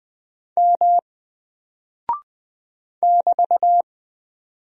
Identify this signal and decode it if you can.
Morse code, heard as a keyed tone, about 20 wpm, 700 Hz. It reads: M <BT>
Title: Single Character (Letter:Number:Punctuation:Prosign) Alt 20wpm